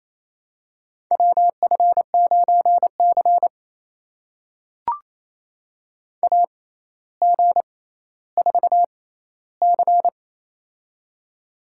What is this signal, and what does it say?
Morse code, heard as a keyed tone, about 28 wpm, spaced as for 14 wpm, 700 Hz. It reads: WF9C AG4C
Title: USA Call Signs - 4 Characters - 6x Character Spacing - 28wpm